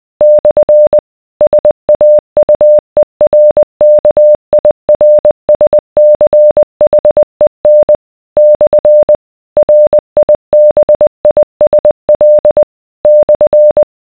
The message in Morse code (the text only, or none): / SAUERKIRSCHEN / RIBISL /